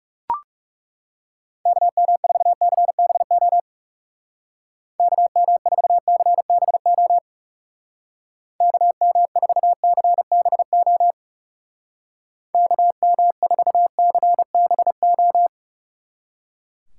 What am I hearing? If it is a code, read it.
Morse code, heard as a keyed tone, about 36 wpm, 700 Hz. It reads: KM4CBO KM4CBO KM4CBO KM4CBO
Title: Speed Racing - USA 2x3 Call Signs 30wpm